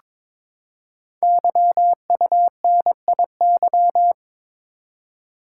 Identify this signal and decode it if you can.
Morse code, heard as a keyed tone, about 22 wpm, 700 Hz. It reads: YUNIY
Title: Code Groups - 5 Letters - 22wpm